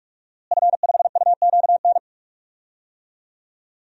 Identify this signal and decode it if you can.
Morse code, heard as a keyed tone, about 42 wpm, 700 Hz. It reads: F5UQN
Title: DXCC-100 Call Signs - 5 Characters - Repeated 3x - 45wpm